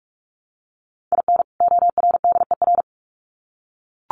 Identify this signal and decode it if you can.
Morse code, heard as a keyed tone, about 44 wpm, 700 Hz. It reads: IN ORDER
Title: Sets of 2 Words 45wpm